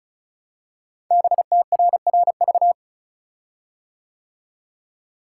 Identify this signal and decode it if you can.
Morse code, heard as a keyed tone, about 36 wpm, 700 Hz. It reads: BTRRV
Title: Code Groups - 5 Letters - 35wpm